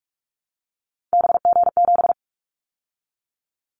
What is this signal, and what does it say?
Morse code, heard as a keyed tone, about 46 wpm, 700 Hz. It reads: BG7
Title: Callsign Prefixes for Top 100 DXCC - 45wpm